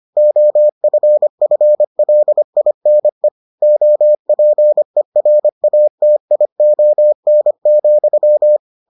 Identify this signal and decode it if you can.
Morse code, heard as a keyed tone, about 25 wpm, 600 Hz. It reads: OFFLINE OPERATION,